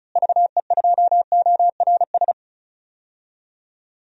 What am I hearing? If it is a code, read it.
Morse code, heard as a keyed tone, about 34 wpm, 700 Hz. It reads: VE2ORS
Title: DXCC-100 Call Signs - 6 Characters - Repeated 3x - 35wpm